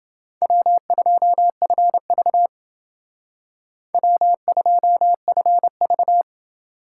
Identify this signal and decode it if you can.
Morse code, heard as a keyed tone, about 28 wpm, 700 Hz. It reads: W2FV W2FV